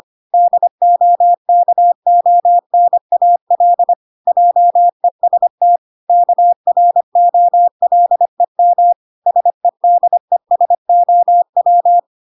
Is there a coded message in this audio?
Yes: Morse code from 0.3 to 12.0 s, heard as a keyed tone, about 25 wpm, 700 Hz. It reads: DOKONAL JEST KROLEM SEDESOW